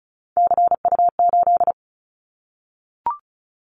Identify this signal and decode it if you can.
Morse code, heard as a keyed tone, about 36 wpm, 700 Hz. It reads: CU8
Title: Callsign Prefixes for Top 50 DXCC - 35wpm